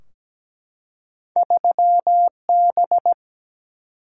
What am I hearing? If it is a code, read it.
Morse code, heard as a keyed tone, about 17 wpm, 700 Hz. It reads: 3B
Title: Single-Letter Single-Number 17wpm